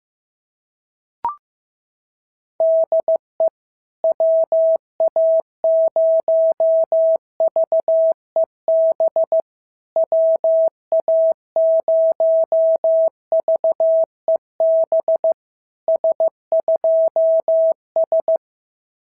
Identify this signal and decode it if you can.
Morse code, heard as a keyed tone, about 14 wpm, 650 Hz. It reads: DE WA0VEB WA0VEB S2S
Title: SOTA QSO Practice 15wpm